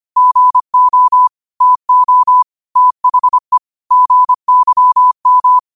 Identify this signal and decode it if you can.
Morse code, heard as a keyed tone, about 25 wpm, 1000 Hz. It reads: GO TO THE GYM
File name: qotdfiles / 25wpm